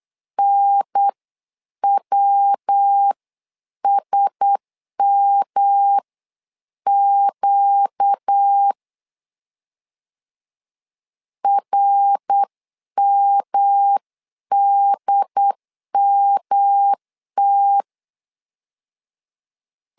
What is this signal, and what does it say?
Morse code, heard as a keyed tone, about 8 wpm, spaced as for 7 wpm, 800 Hz. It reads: NWSMQ RMDMT
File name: LEZIONI / 4 Q1S 8wpm 2710